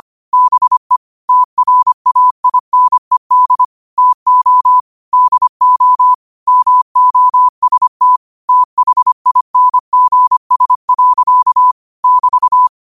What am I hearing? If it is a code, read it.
Morse code, heard as a keyed tone, about 25 wpm, 1000 Hz. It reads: BE TRAINED TO DO MOST THINGS. <BT>
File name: qotdfiles / 25wpm